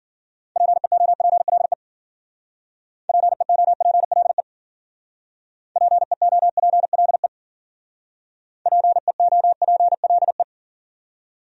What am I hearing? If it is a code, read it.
Morse code, heard as a keyed tone, about 51 wpm, 700 Hz. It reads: PEOPLE PEOPLE PEOPLE PEOPLE